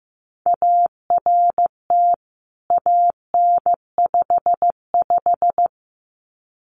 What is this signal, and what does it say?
Morse code, heard as a keyed tone, about 15 wpm, 700 Hz. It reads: ART AN55
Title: CWT Exchange 15wpm